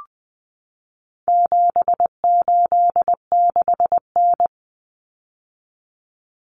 Morse code, heard as a keyed tone, about 20 wpm, 700 Hz. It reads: 786N
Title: Common Cut Numbers - 4-Digit 20wpm